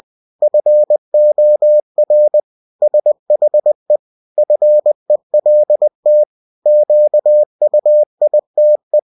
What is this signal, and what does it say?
Morse code, heard as a keyed tone, about 20 wpm, 600 Hz. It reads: FOR SHE FELT QUITE
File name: cw-books / winnie-the-pooh / winnie-20wpm-page-0071